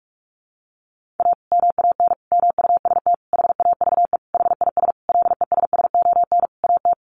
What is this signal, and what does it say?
Morse code, heard as a keyed tone, about 44 wpm, 700 Hz. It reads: A MAN MUST HAVE HIS LESSON AT
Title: Sentences from Top 1200 Words 45wpm